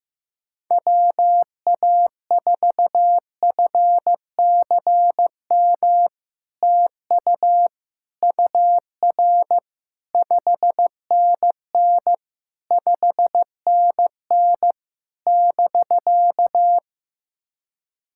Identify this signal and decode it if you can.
Morse code, heard as a keyed tone, about 14 wpm, 700 Hz. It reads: WA4FCM TU UR 5NN 5NN <BK>
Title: SOTA QSO Practice 15wpm